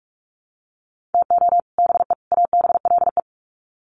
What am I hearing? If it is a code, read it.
Morse code, heard as a keyed tone, about 44 wpm, 700 Hz. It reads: TO BE ABLE
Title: Sets of 3 Words 45wpm